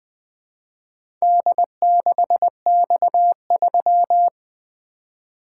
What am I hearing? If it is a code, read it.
Morse code, heard as a keyed tone, about 20 wpm, 700 Hz. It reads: D6X3
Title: Four Letter-Number Combinations - 20wpm